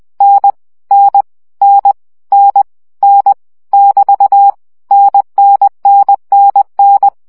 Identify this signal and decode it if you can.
Morse code, heard as a keyed tone, about 20 wpm, 800 Hz. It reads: N N N N N <BT> NNNNN
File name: CW gyakorlas n 17 baud